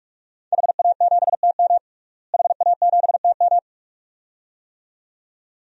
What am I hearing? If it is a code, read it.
Morse code, heard as a keyed tone, about 47 wpm, 700 Hz. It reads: HA7TM HA7TM
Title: DXCC-100 Call Signs - 5 Characters - Repeated 2x - 45wpm